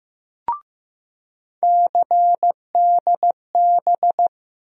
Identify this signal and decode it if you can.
Morse code, heard as a keyed tone, about 15 wpm, 700 Hz. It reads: CDB
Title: USA Call Sign Postfixes - 3 Letters 15wpm